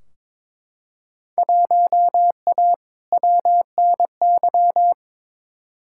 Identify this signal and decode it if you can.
Morse code, heard as a keyed tone, about 21 wpm, 700 Hz. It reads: 1A WNY
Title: Field Day Exchanges Encore - 22wpm